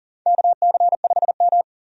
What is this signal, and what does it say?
Morse code, heard as a keyed tone, about 40 wpm, 700 Hz. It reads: KC5M
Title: USA Call Signs - 4 Characters - 40wpm